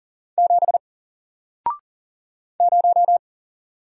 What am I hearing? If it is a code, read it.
Morse code, heard as a keyed tone, about 40 wpm, 700 Hz. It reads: 7 0
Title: Single Letter-Number 40wpm